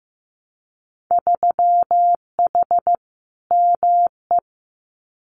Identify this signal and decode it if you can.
Morse code, heard as a keyed tone, about 15 wpm, 700 Hz. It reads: 3H ME
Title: Winter Field Day Exchanges Encore 15wpm